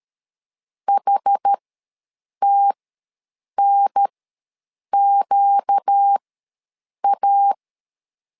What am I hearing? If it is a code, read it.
Morse code, heard as a keyed tone, about 13 wpm, spaced as for 7 wpm, 800 Hz. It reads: HTNQA
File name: LEZIONI / 4 Q ws 12wpm 1912